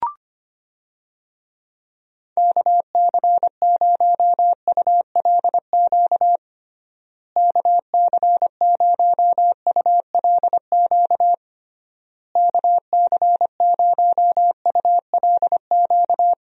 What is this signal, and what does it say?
Morse code, heard as a keyed tone, about 25 wpm, 700 Hz. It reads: KC0ULQ KC0ULQ KC0ULQ
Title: USA Call Signs - 6 Characters - Repeated 3x - 25wpm